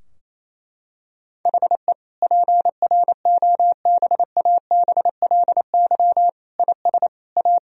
Morse code, heard as a keyed tone, about 28 wpm, 700 Hz. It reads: HE PROBABLY IS A
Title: Sentences from Top 1000 Words - Review Part 4 - 28wpm